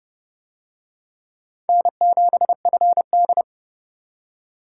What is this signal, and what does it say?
Morse code, heard as a keyed tone, about 27 wpm, 700 Hz. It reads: N7FD